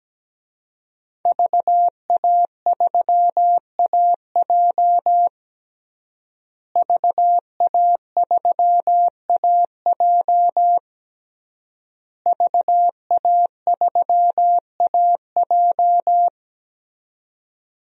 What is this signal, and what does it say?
Morse code, heard as a keyed tone, about 17 wpm, 700 Hz. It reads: VA3AJ VA3AJ VA3AJ
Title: DXCC-100 Call Signs - 5 Characters - Repeated 3x - 17wpm